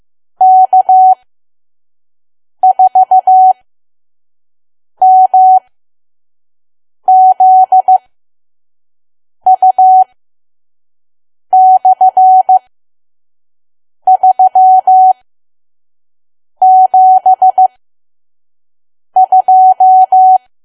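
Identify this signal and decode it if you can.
Morse code, heard as a keyed tone, about 15 wpm, spaced as for 5 wpm, 750 Hz. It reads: K4MZU/372
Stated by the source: Morse code practice at 5 WPM from KQ2P.COM.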